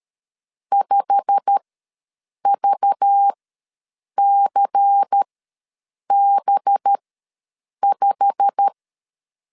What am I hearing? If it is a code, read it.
Morse code, heard as a keyed tone, about 13 wpm, spaced as for 7 wpm, 800 Hz. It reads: 5VCB5
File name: LEZIONI / 8 YJ45 / 8 YJ45 ws 12wpm 3146